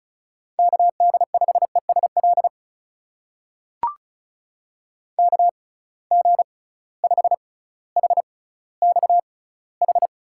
Morse code, heard as a keyed tone, about 34 wpm, spaced as for 18 wpm, 700 Hz. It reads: KD5ESL KG5HXH
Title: USA Call Signs - 6x Character Spacing - 35wpm